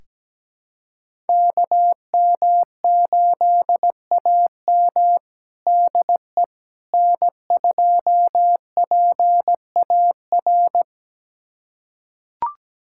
Morse code, heard as a keyed tone, about 17 wpm, 700 Hz. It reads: KM8AM DE N2PAR
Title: USA Callsign DE Callsign - 5 Characters - 17wpm